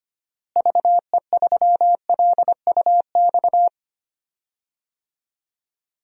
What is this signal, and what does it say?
Morse code, heard as a keyed tone, about 25 wpm, 700 Hz. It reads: VE3LUX
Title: DXCC-100 Call Signs - 6 Characters - Repeated 3x - 25wpm